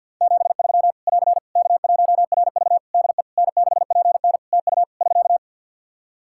Morse code, heard as a keyed tone, about 50 wpm, 700 Hz. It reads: 73 <AR> K1RV DE N6PN TU <SK>
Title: Practice QSO - Exchange 4 - Ending - 50wpm